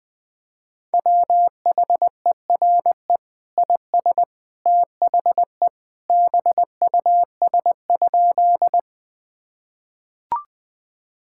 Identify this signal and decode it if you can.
Morse code, heard as a keyed tone, about 20 wpm, 700 Hz. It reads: WHERE IS THE BUS?
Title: Sentences from Top 1000 Words - Review Part 6 - 20wpm